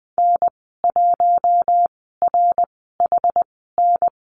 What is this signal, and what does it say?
Morse code, heard as a keyed tone, about 20 wpm, spaced as for 14 wpm, 700 Hz. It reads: N1RHN
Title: 069 Lesson 12g - Letter d - Call Signs 20wpm